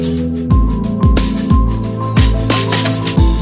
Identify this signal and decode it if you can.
There is no code.